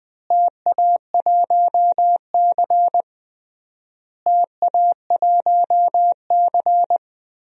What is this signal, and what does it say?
Morse code, heard as a keyed tone, about 20 wpm, 700 Hz. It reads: TA1C TA1C
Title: DXCC-100 Call Signs - 4 Characters - Repeated 3x - 20wpm